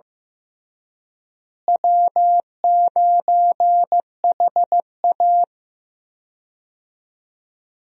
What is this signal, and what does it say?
Morse code, heard as a keyed tone, about 16 wpm, 700 Hz. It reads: W9HA
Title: USA Call Signs - 4 Characters - Repeated 3x - 15wpm